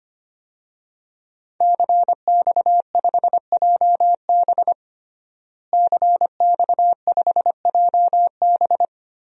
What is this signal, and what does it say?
Morse code, heard as a keyed tone, about 25 wpm, 700 Hz. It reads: CX5JB CX5JB
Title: DXCC-100 Call Signs - 5 Characters - Repeated 3x - 25wpm